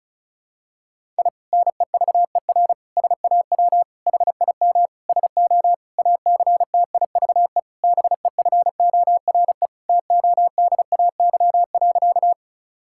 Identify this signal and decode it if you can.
Morse code, heard as a keyed tone, about 36 wpm, 700 Hz. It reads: I NEVER SAW HIM SO ACTIVE BEFORE TODAY.
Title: Sentences from Top 1000 Words - Review Part 6 - 35wpm